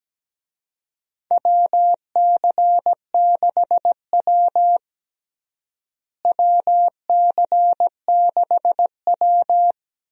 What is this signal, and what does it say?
Morse code, heard as a keyed tone, about 17 wpm, 700 Hz. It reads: WC6W WC6W
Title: USA Call Signs - 4 Characters - Repeated 2x - 17wpm